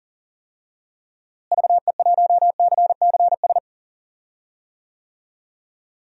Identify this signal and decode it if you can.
Morse code, heard as a keyed tone, about 40 wpm, 700 Hz. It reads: VE1CCS
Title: DXCC-100 Call Signs - 6 Characters - Repeated 2x - 40wpm